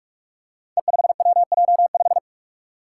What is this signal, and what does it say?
Morse code, heard as a keyed tone, about 44 wpm, 700 Hz. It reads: E5WJ5
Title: Five Challenging Letter-Number Combinations 45wpm